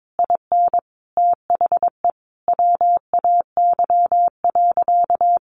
Morse code, heard as a keyed tone, about 22 wpm, 700 Hz. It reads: IN THE WAY.
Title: Sentences from Top 200 Words 22wpm